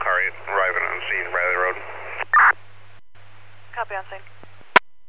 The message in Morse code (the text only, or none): none